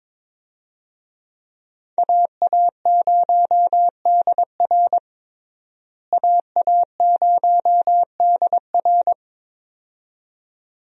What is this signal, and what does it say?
Morse code, heard as a keyed tone, about 21 wpm, 700 Hz. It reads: AA0DR AA0DR
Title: USA Call Signs - 5 Characters - Repeated 2x - 22wpm